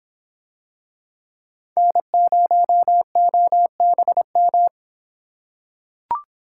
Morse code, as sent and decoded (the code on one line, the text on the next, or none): -. ----- --- -... --
N0OBM